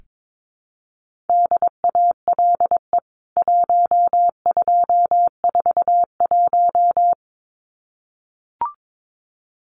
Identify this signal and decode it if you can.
Morse code, heard as a keyed tone, about 22 wpm, 700 Hz. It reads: DALE 1241
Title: CWT Exchange 22wpm